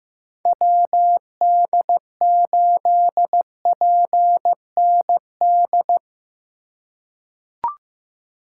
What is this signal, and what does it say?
Morse code, heard as a keyed tone, about 15 wpm, 700 Hz. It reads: WD8PND
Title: USA Call Signs - 6 Characters - 4x Character Spacing - 15wpm